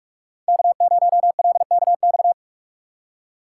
Morse code, heard as a keyed tone, about 45 wpm, 700 Hz. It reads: K0LKX
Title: USA Call Signs - 5 Characters - 3x Character Spacing - 45wpm